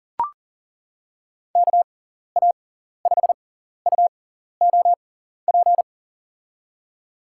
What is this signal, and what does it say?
Morse code, heard as a keyed tone, about 40 wpm, spaced as for 14 wpm, 700 Hz. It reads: KA5UOP
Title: USA Call Signs - 6 Characters - 6x Character Spacing - 40wpm